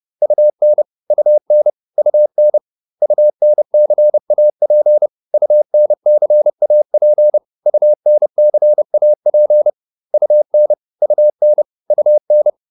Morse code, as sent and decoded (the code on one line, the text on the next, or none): ..- -. / ..- -. / ..- -. / ..- -. -.-. .- .--. / ..- -. -.-. .- .--. / ..- -. -.-. .- .--. / ..- -. / ..- -. / ..- -.
UN UN UN UNCAP UNCAP UNCAP UN UN UN